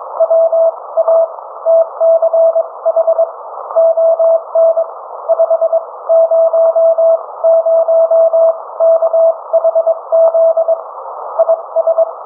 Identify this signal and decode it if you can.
Morse code, heard as a keyed tone, about 22 wpm, spaced as for 18 wpm, 660 Hz. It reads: WATCH ON 500KHZ IS